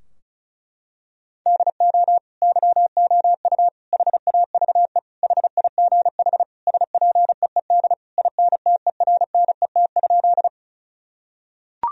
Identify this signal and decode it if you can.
Morse code, heard as a keyed tone, about 34 wpm, 700 Hz. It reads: DO YOU HAVE HIGH SPEED INTERNET?
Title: Sentences from Top 1000 Words - Review Part 4 - 35wpm